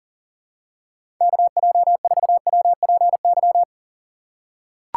Morse code, heard as a keyed tone, about 37 wpm, 700 Hz. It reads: KJ4WPY
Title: USA Call Signs - 6 Characters - 2x Character Spacing - 40wpm